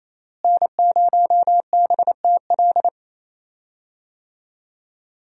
Morse code, heard as a keyed tone, about 28 wpm, 700 Hz. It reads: N0BTL
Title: USA Call Signs - 5 Characters - Repeated 3x - 28wpm